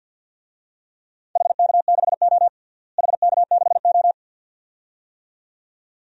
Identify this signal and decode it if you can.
Morse code, heard as a keyed tone, about 51 wpm, 700 Hz. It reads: HK6O HK6O
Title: DXCC-100 Call Signs - 4 Characters - Repeated 2x - 50wpm